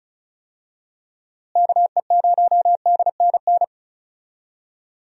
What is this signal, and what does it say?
Morse code, heard as a keyed tone, about 36 wpm, 700 Hz. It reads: KE0DNN
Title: USA Call Signs - 6 Characters - Repeated 2x - 35wpm